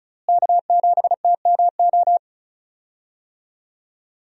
Morse code, heard as a keyed tone, about 34 wpm, 700 Hz. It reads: K7TMO
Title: USA Call Signs - 5 Characters - Repeated 2x - 35wpm